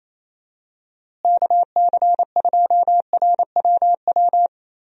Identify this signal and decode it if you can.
Morse code, heard as a keyed tone, about 28 wpm, 700 Hz. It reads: KC2RWW
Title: USA Call Signs - 6 Characters - Repeated 2x - 28wpm